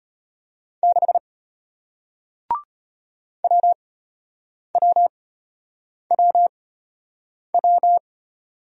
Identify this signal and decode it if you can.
Morse code, heard as a keyed tone, about 34 wpm, spaced as for 8 wpm, 700 Hz. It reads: 6 WWWW